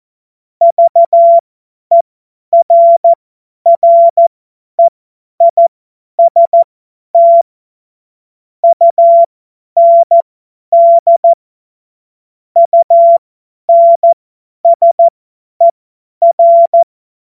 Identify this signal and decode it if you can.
Morse code, heard as a keyed tone, about 14 wpm, spaced as for 10 wpm, 680 Hz. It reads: VERREIST UND UNSER
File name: Morsekurs / Witze / Witz 22. Februar 2022 50 F70